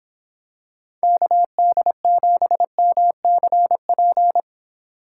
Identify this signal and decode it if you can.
Morse code, heard as a keyed tone, about 26 wpm, 700 Hz. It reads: KD7MCP